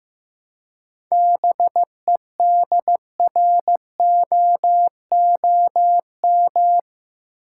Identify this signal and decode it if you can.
Morse code, heard as a keyed tone, about 15 wpm, 700 Hz. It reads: BEDROOM
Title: Seven Letter Words - 15wpm